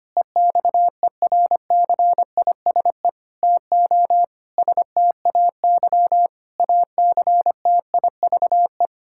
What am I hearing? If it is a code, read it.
Morse code, heard as a keyed tone, about 25 wpm, 700 Hz. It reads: EXERCISE TO STAY ACTIVE
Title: Sentences from Top 1000 Words - Review Part 6 - 25wpm